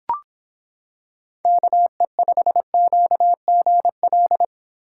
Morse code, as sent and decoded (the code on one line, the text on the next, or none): -.- . ..... --.- --. .-..
KE5QGL